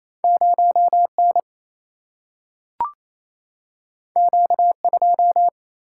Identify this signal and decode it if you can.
Morse code, heard as a keyed tone, about 28 wpm, 700 Hz. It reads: 0 N Q 2
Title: Single-Letter Single-Number 28wpm